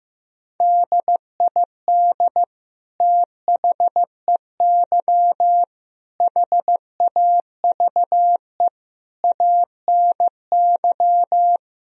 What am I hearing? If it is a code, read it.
Morse code, heard as a keyed tone, about 15 wpm, 700 Hz. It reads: DID THEY HAVE ANY
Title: Sentences from Top 1000 Words - Review Part 3 - 15wpm